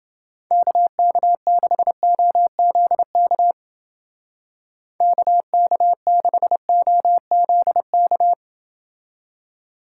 Morse code, heard as a keyed tone, about 28 wpm, 700 Hz. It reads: KK6OZK KK6OZK